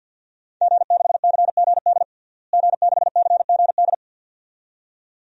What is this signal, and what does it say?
Morse code, heard as a keyed tone, about 52 wpm, 700 Hz. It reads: G6CGD G6CGD
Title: DXCC-100 Call Signs - 5 Characters - Repeated 2x - 50wpm